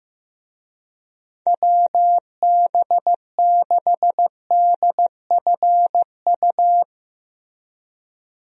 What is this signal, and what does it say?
Morse code, heard as a keyed tone, about 15 wpm, 700 Hz. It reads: WB6DFU
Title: USA Call Signs - 6 Characters - Repeated 2x - 15wpm